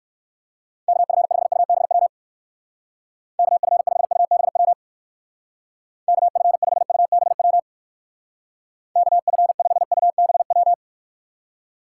Morse code, heard as a keyed tone, about 59 wpm, 700 Hz. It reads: KF5UBW KF5UBW KF5UBW KF5UBW